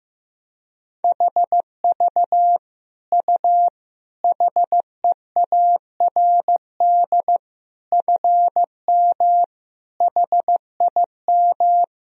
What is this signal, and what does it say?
Morse code, heard as a keyed tone, about 16 wpm, 700 Hz. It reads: HV U HEARD FM HIM
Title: Sentences from Top 400 Words and Abbreviations 15wpm